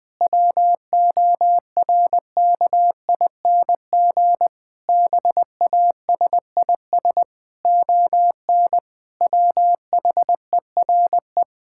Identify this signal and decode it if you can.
Morse code, heard as a keyed tone, about 21 wpm, 700 Hz. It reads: WORKING BASIS ON WHERE
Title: Sentences from Top 1000 Words - Review Part 5 - 20wpm